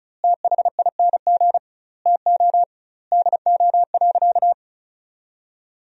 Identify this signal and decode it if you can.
Morse code, heard as a keyed tone, about 34 wpm, 700 Hz. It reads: THING TO DO.
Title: 2x Word-Spacing - Sentences from Top 100 Words - 35wpm